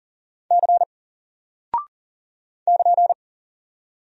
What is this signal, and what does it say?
Morse code, heard as a keyed tone, about 40 wpm, 700 Hz. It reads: C <KN>
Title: Single Character (Letter:Number:Punctuation:Prosign) 40wpm